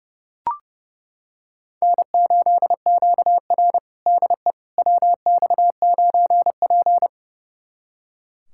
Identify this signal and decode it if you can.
Morse code, heard as a keyed tone, about 30 wpm, 700 Hz. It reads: N8QR DE WX9P
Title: USA Callsign DE Callsign - 4 Characters - 30wpm